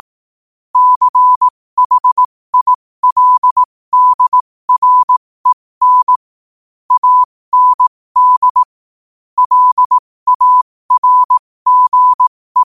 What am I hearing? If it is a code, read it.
Morse code, heard as a keyed tone, about 18 wpm, spaced as for 15 wpm, 1000 Hz. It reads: CHILDREN AND LARGE
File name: qotdfiles / 15wpm